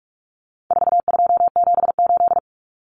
Morse code, heard as a keyed tone, about 44 wpm, 700 Hz. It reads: 4278
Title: Numbers - 4-Digit 45wpm